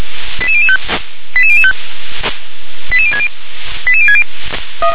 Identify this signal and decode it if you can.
Five-tone selective call (ZVEI), heard as quick runs of five tones, tones of about 70 ms. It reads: 88005 88005 88060 88060